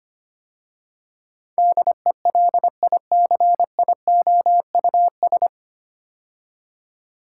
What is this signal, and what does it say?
Morse code, heard as a keyed tone, about 25 wpm, 700 Hz. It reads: DELICIOUS
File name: practice / 25wpm / 9 Letter Words - First 3 Letters Given - 25wpm